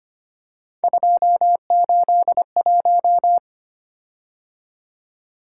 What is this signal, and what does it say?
Morse code, heard as a keyed tone, about 25 wpm, 700 Hz. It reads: 281
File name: Three Letter-Number Combinations - 25wpm